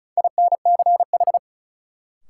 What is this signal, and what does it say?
Morse code, heard as a keyed tone, about 36 wpm, 700 Hz. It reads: INCH
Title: Sentences from Top 1000 Words - Review Part 3 - New Word Drill - 35wpm